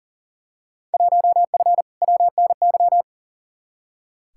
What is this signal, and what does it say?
Morse code, heard as a keyed tone, about 40 wpm, 700 Hz. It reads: 1F WNY
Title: Field Day Exchanges - 40wpm